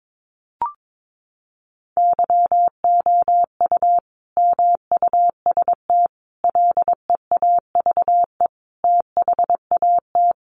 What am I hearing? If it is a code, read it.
Morse code, heard as a keyed tone, about 22 wpm, 700 Hz. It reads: YOU MUST LEAVE THAT
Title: Sentences from Top 400 Words 22wpm